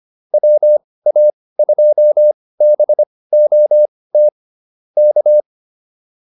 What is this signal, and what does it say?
Morse code, heard as a keyed tone, about 25 wpm, spaced as for 18 wpm, 600 Hz. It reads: WA2BOT K